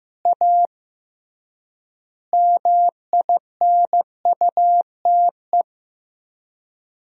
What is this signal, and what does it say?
Morse code, heard as a keyed tone, about 15 wpm, 700 Hz. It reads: A MINUTE
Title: 3x Word-Spacing - Sentences from Top 500 Words - 15wpm